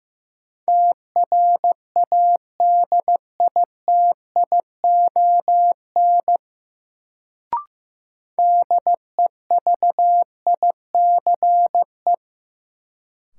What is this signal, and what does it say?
Morse code, heard as a keyed tone, about 15 wpm, 700 Hz. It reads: TRADITION DEVICE
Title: Sentences from Top 1500 Words - New Word Drill Alt - 15wpm